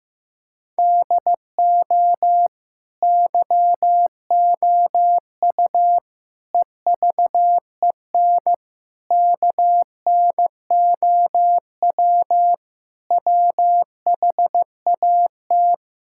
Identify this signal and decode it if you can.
Morse code, heard as a keyed tone, about 15 wpm, 700 Hz. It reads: DO YOU EVEN KNOW WHAT
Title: Sentences from Top 1000 Words - Review Part 5 - 15wpm